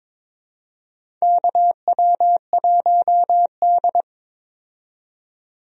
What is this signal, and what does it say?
Morse code, heard as a keyed tone, about 22 wpm, 700 Hz. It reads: KW1D
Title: USA Call Signs - 4 Characters - 22wpm